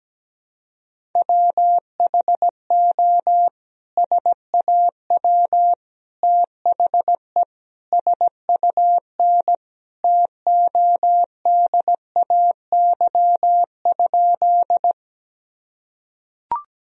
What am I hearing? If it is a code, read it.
Morse code, heard as a keyed tone, about 17 wpm, 700 Hz. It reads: WHO SAW THE SUN TODAY?
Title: Sentences from Top 1000 Words - Review Part 1 - 17wpm